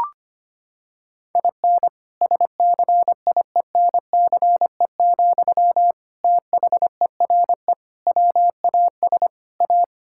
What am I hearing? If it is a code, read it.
Morse code, heard as a keyed tone, about 25 wpm, 700 Hz. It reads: IN SCIENCE, THERE WAS A
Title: Sentences from Top 1000 Words - Review Part 5 - 25wpm